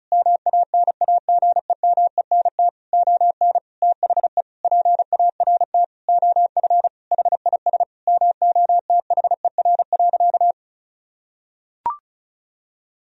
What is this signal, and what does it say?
Morse code, heard as a keyed tone, about 36 wpm, 700 Hz. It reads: MANAGEMENT ON THE PART OF HIS MOTHER.
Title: Sentences from Top 1000 Words - Review Part 4 - 35wpm